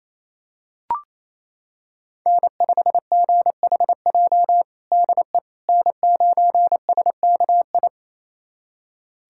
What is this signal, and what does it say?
Morse code, heard as a keyed tone, about 28 wpm, 690 Hz. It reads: N5GHJ DE N9SKI
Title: USA Callsign DE Callsign - 5 Characters - 28wpm